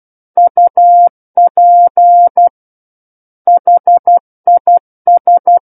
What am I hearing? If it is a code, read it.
Morse code, heard as a keyed tone, about 12 wpm, 700 Hz. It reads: UP HIS
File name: Book One / 03-Chapter 3 - 12 WPM